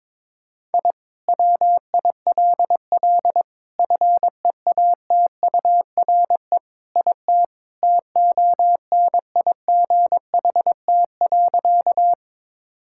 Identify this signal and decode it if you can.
Morse code, heard as a keyed tone, about 22 wpm, 700 Hz. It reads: I WILL FEATURE IT TONIGHT.
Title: Sentences from Top 1000 Words - Review Part 5 - 22wpm